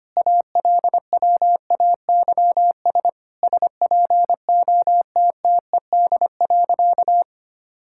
Morse code, heard as a keyed tone, about 25 wpm, 700 Hz. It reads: ALWAYS SPOTTED.
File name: practice / 25wpm / All-Age Appropriate Jokes - 25wpm